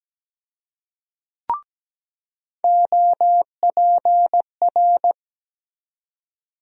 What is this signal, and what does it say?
Morse code, heard as a keyed tone, about 17 wpm, 700 Hz. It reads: OPR
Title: USA Call Sign Postfixes - 3 Letters 17wpm